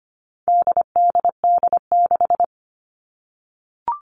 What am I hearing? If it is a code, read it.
Morse code, heard as a keyed tone, about 25 wpm, 700 Hz. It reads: DDD6
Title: All Cut Numbers - 4-Digit 25wpm